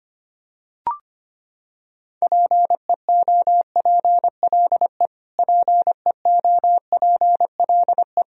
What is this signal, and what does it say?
Morse code, heard as a keyed tone, about 25 wpm, 700 Hz. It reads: PEOPLE PEOPLE
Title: Top 100 Words Repeated 25wpm